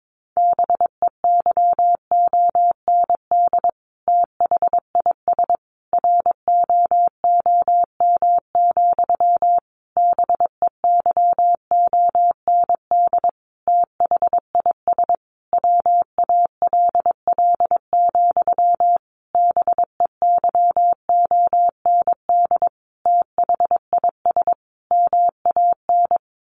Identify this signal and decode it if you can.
Morse code, heard as a keyed tone, about 22 wpm, 700 Hz. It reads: BEYOND THIS ROOM, BEYOND THIS WALL, BEYOND THIS MAN
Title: A Sound Of Thunder 22wpm